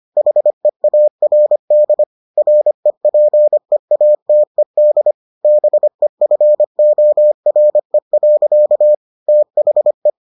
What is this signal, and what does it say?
Morse code, heard as a keyed tone, about 25 wpm, 600 Hz. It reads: HEARD REPEATED BEFORE. THE